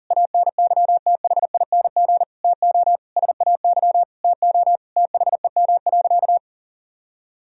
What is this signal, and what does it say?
Morse code, heard as a keyed tone, about 37 wpm, 700 Hz. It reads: ANYTHING TO SAY TO THEM.
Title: Sentences from Top 1000 Words - Review Part 1 - 40wpm